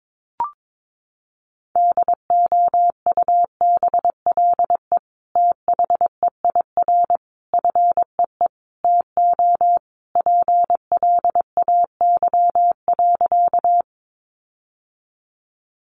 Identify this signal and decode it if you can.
Morse code, heard as a keyed tone, about 22 wpm, 700 Hz. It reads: DOUBLE THEIR FEE TO PLAY.
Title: Sentences from Top 1500 Words 22wpm